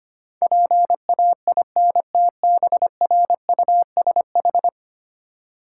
Morse code, heard as a keyed tone, about 28 wpm, 700 Hz. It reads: PAINTBRUSH
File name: Easy Words - Speed-Racing with Hints - 25wpm